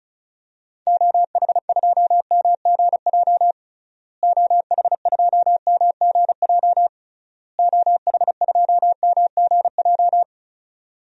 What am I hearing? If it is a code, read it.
Morse code, heard as a keyed tone, about 36 wpm, 700 Hz. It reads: OH2MGJ OH2MGJ OH2MGJ
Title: DXCC-100 Call Signs - 6 Characters - Repeated 3x - 35wpm